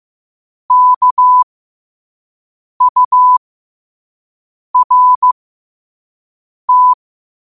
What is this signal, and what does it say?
Morse code, heard as a keyed tone, about 15 wpm, spaced as for 5 wpm, 1000 Hz. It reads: KURT
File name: qotdfiles / 5wpm